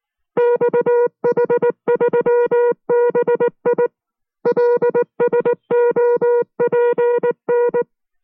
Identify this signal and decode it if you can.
Morse code, heard as a keyed tone, about 19 wpm, 480 Hz. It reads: XH3BI LSOPN